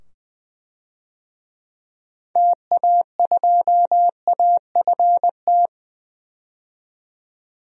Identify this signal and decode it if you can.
Morse code, heard as a keyed tone, about 21 wpm, 700 Hz. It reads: TA2AFT
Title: DXCC-100 Call Signs - 6 Characters - 20wpm